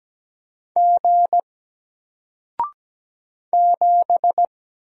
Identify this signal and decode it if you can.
Morse code, heard as a keyed tone, about 17 wpm, 700 Hz. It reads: G 7
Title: Single Character (Letter:Number:Punctuation:Prosign) Alt 17wpm